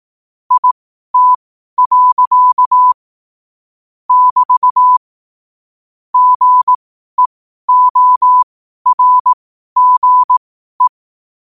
Morse code, heard as a keyed tone, about 18 wpm, spaced as for 12 wpm, 1000 Hz. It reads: IT. <BT> GEORGE
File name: qotdfiles / 12wpm